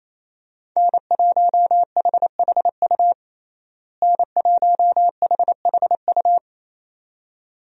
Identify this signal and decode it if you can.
Morse code, heard as a keyed tone, about 28 wpm, 700 Hz. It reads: N1HHU N1HHU
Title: USA Call Signs - 5 Characters - Repeated 2x - 28wpm